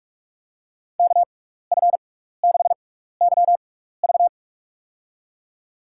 Morse code, heard as a keyed tone, about 46 wpm, spaced as for 16 wpm, 700 Hz. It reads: KF6YV
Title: USA Call Signs - 5 Characters - 6x Character Spacing - 45wpm